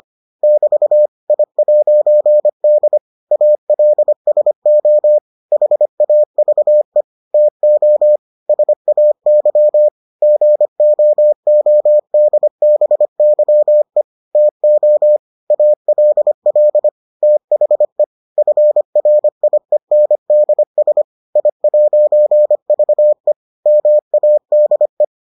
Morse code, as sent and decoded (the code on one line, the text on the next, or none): -...- / .. .----. -.. / .- .-.. ... --- / .... .- ...- . / - --- / ... .- -.-- / --. --- --- -.. -... -.-- . / - --- / .- .-.. .-.. / - .... / ..-. .-. .. -. -.. ... / .. .----. ...- . / -- .- -.. .
<BT> I'D ALSO HAVE TO SAY GOODBYE TO ALL THE FRIENDS I'VE MADE